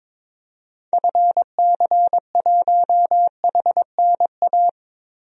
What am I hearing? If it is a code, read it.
Morse code, heard as a keyed tone, about 22 wpm, 700 Hz. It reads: FC1HNA
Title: DXCC-100 Call Signs - 6 Characters - 22wpm